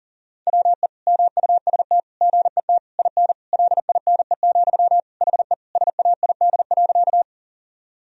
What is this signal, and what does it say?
Morse code, heard as a keyed tone, about 40 wpm, 700 Hz. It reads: WE MUST GET IN LINE, HE SAID.
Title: Sentences from Top 1000 Words - Review Part 1 - 40wpm